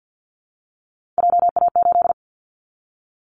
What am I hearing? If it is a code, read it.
Morse code, heard as a keyed tone, about 50 wpm, 700 Hz. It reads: JA8
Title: Callsign Prefixes for Top 25 DXCC - Repeated 3x - 50wpm